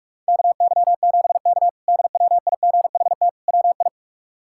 Morse code, heard as a keyed tone, about 46 wpm, 700 Hz. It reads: KY7K DWIGHT WI
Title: K1USN SST QSO Practice - 45wpm